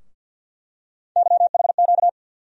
Morse code, heard as a keyed tone, about 48 wpm, 700 Hz. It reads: YHQ
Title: USA Call Sign Postfixes - 3 Letters 50wpm